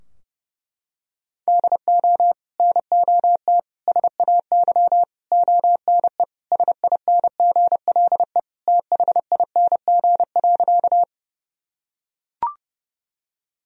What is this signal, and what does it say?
Morse code, heard as a keyed tone, about 32 wpm, 700 Hz. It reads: DO NOT SAY ONE SINGLE THING.
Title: Sentences from Top 1000 Words - Review Part 4 - 30wpm